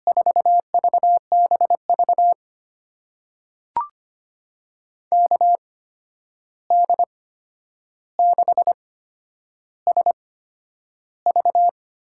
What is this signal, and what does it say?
Morse code, heard as a keyed tone, about 25 wpm, spaced as for 10 wpm, 700 Hz. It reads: 4VBV KD6SV
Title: USA Call Signs - 6 Characters - 8x Character Spacing - 25wpm